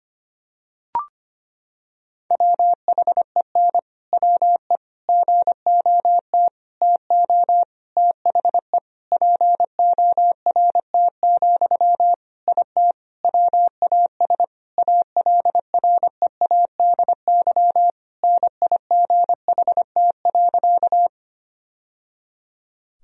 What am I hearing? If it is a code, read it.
Morse code, heard as a keyed tone, about 25 wpm, 700 Hz. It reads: WHEN WE GOT TO THE PORT, IT WAS ALREADY NIGHT.
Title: Sentences from Top 1000 Words - Review Part 1 - 25wpm